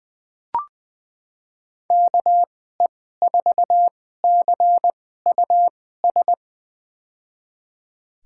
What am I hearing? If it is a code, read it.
Morse code, heard as a keyed tone, about 20 wpm, spaced as for 15 wpm, 700 Hz. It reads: KE4CUS
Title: USA Call Signs - 6 Characters - 2x Character Spacing - 20wpm